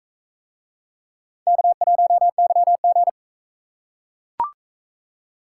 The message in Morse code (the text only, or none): K1YG